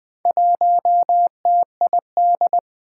Morse code, heard as a keyed tone, about 20 wpm, 700 Hz. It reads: 1TID
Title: Four Challenging Letter-Number Combinations 20wpm